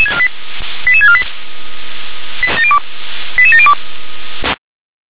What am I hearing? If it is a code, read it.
Five-tone selective call (ZVEI), heard as quick runs of five tones, tones of about 80 ms. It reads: 88648 88702
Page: SDR/ZVEI